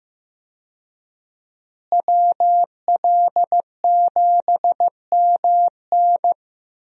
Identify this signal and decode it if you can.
Morse code, heard as a keyed tone, about 15 wpm, 700 Hz. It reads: WL7MN
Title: USA Call Signs - 5 Characters - Repeated 3x - 15wpm